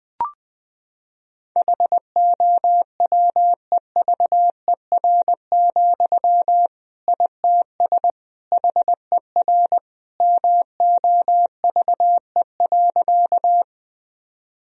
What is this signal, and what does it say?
Morse code, heard as a keyed tone, about 20 wpm, 700 Hz. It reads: HOWEVER, ITS HER MOVE.
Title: Sentences from Top 1000 Words - Review Part 1 - 20wpm